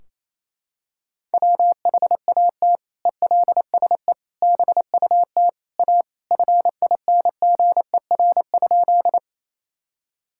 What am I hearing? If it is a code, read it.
Morse code, heard as a keyed tone, about 27 wpm, 700 Hz. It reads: WHAT ELSE BUT A FINGER?
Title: Sentences from Top 1600 Words 28wpm